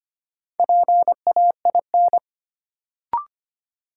Morse code, heard as a keyed tone, about 25 wpm, 700 Hz. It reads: PAIN
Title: Sentences from Top 1600 Words - New Word Drill Alt - 25wpm